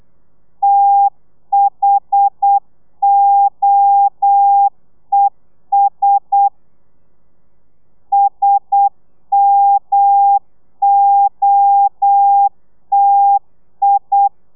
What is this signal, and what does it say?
Morse code, heard as a keyed tone, about 8 wpm, 800 Hz. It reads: THOES SMOTI
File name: telegrafia / i2tfj / Lez1 35